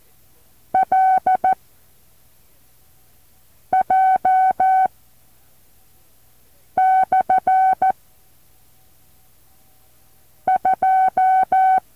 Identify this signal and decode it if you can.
Morse code, heard as a keyed tone, about 14 wpm, 760 Hz. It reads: L J / 2
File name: intl morse code2